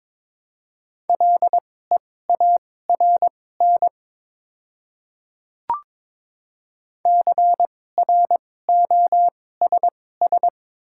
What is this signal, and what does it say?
Morse code, heard as a keyed tone, about 22 wpm, spaced as for 16 wpm, 700 Hz. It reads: LEARN CROSS
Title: 2x Character Spacing - Top 300 Words - 22wpm